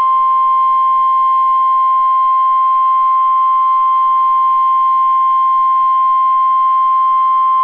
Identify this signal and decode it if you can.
Emergency Alert System (SAME) tones (the 1050 Hz attention tone). It is heard from 0.6 to 7.1 s.